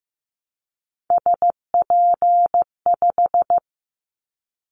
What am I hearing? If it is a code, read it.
Morse code, heard as a keyed tone, about 15 wpm, 700 Hz. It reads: SP5
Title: Callsign Prefixes for Top 25 DXCC - 15wpm